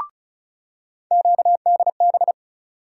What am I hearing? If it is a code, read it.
Morse code, heard as a keyed tone, about 36 wpm, 700 Hz. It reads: QDB
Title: USA Call Sign Postfixes - 3 Letters 35wpm